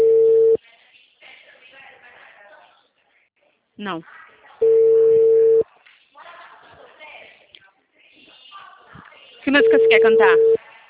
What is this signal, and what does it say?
Ringback tone (China, 450 Hz), with speech.